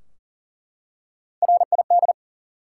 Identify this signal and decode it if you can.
Morse code, heard as a keyed tone, about 40 wpm, 700 Hz. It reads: RID
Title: Three Letter Words - 40wpm